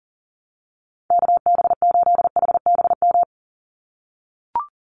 Morse code, heard as a keyed tone, about 40 wpm, 700 Hz. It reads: KB8HBM
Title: USA Call Signs 40wpm